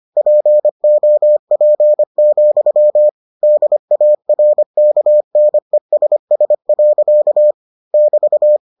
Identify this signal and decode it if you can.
Morse code, heard as a keyed tone, about 25 wpm, 600 Hz. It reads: POP, DARKNESS. <BT>